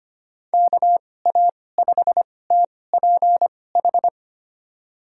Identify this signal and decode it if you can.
Morse code, heard as a keyed tone, about 25 wpm, spaced as for 18 wpm, 700 Hz. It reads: KA5TPH
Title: USA Call Signs - 6 Characters - 2x Character Spacing - 25wpm